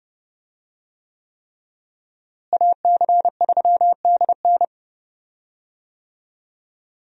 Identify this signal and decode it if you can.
Morse code, heard as a keyed tone, about 30 wpm, 700 Hz. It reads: AC3DN
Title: USA Call Signs - 5 Characters - Repeated 3x - 30wpm